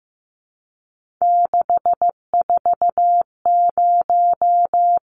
Morse code, heard as a keyed tone, about 15 wpm, 700 Hz. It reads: 640
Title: Numbers - 3-Digit 15wpm